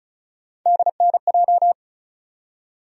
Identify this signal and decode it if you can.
Morse code, heard as a keyed tone, about 36 wpm, 700 Hz. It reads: DNJ
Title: USA Call Sign Postfixes - 3 Letters 35wpm